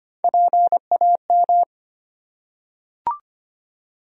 Morse code, heard as a keyed tone, about 25 wpm, 700 Hz. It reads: PAM
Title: USA Common Names - Max 3 Letters - 25wpm